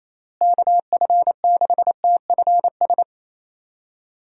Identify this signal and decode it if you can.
Morse code, heard as a keyed tone, about 28 wpm, 700 Hz. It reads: KF6TFS
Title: USA Call Signs - 6 Characters - Repeated 2x - 28wpm